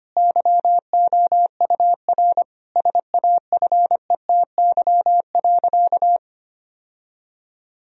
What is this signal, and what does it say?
Morse code, heard as a keyed tone, about 25 wpm, 700 Hz. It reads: YOUR SAFETY.
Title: Sentences from Top 1000 Words - Review Part 4 - 25wpm